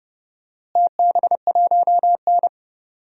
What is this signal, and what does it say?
Morse code, heard as a keyed tone, about 30 wpm, 700 Hz. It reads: TB1N
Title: DXCC-100 Call Signs - 4 Characters - Repeated 2x - 30wpm